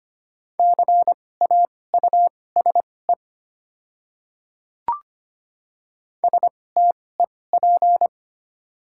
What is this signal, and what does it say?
Morse code, heard as a keyed tone, about 25 wpm, spaced as for 18 wpm, 700 Hz. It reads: CAUSE STEP
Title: 2x Character Spacing - Top 500 Words - 25wpm